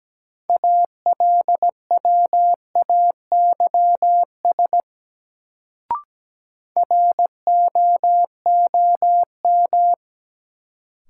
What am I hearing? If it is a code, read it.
Morse code, heard as a keyed tone, about 17 wpm, 700 Hz. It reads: ALWAYS ROOM
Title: Sentences from Top 300 Words - New Word Drill - 17wpm